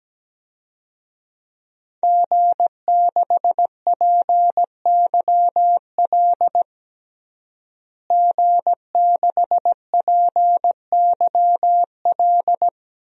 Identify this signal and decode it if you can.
Morse code, heard as a keyed tone, about 17 wpm, 700 Hz. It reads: G6PYL G6PYL
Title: DXCC-100 Call Signs - 5 Characters - Repeated 2x - 17wpm